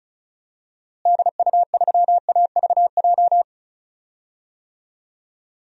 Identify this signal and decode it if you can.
Morse code, heard as a keyed tone, about 36 wpm, 700 Hz. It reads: DU3AVJ
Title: DXCC-100 Call Signs - 6 Characters - Repeated 2x - 35wpm